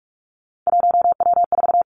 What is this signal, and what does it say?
Morse code, heard as a keyed tone, about 46 wpm, 700 Hz. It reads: 1W4
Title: Three Challenging Letter-Number Combinations 45wpm